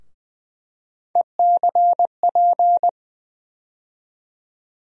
Morse code, heard as a keyed tone, about 20 wpm, 700 Hz. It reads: ECP
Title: USA Call Sign Postfixes - 3 Letters 20wpm